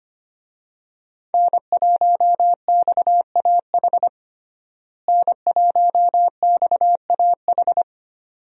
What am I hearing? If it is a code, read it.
Morse code, heard as a keyed tone, about 25 wpm, 700 Hz. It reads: N1XAH N1XAH
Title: USA Call Signs - 5 Characters - Repeated 2x - 25wpm